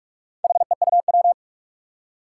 Morse code, heard as a keyed tone, about 46 wpm, 700 Hz. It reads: HEUW
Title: Four Challenging Letter-Number Combinations 45wpm